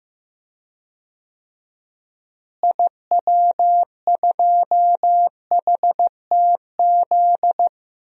Morse code, heard as a keyed tone, about 15 wpm, 700 Hz. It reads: IW2HTZ
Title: DXCC-100 Call Signs - 6 Characters - Repeated 3x - 15wpm